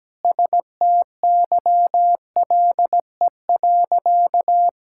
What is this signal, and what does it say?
Morse code, heard as a keyed tone, about 17 wpm, 700 Hz. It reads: STYLE.
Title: Sentences from Top 1000 Words - Review Part 5 - 17wpm